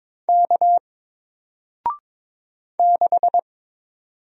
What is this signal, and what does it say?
Morse code, heard as a keyed tone, about 22 wpm, 700 Hz. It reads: K 6
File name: Single-Letter-Number-22wpm